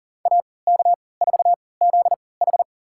Morse code, heard as a keyed tone, about 40 wpm, spaced as for 23 wpm, 700 Hz. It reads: AK4ZH
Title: USA Call Signs - 5 Characters - 3x Character Spacing - 40wpm